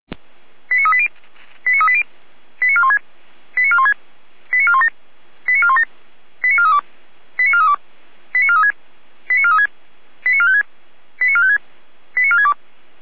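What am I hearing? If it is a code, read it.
Five-tone selective call (ZVEI), heard as quick runs of five tones, tones of about 70 ms. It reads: 89380 89380 89416 89416 89417 89417 89432 89432 89436 89436 89456 89456 89462
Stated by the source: ZVEI five-tone selective call received on SDR.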